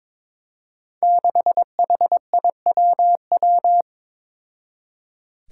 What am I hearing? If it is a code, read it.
Morse code, heard as a keyed tone, about 23 wpm, 700 Hz. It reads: 6HIWW
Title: Five Challenging Letter-Number Combinations 22wpm